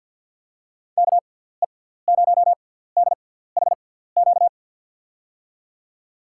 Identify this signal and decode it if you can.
Morse code, heard as a keyed tone, about 51 wpm, spaced as for 17 wpm, 700 Hz. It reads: KE0DHQ
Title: USA Call Signs - 6x Character Spacing - 50wpm